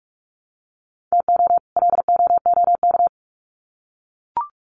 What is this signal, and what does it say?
Morse code, heard as a keyed tone, about 44 wpm, 700 Hz. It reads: TO LOOK
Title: Sets of 2 Words 45wpm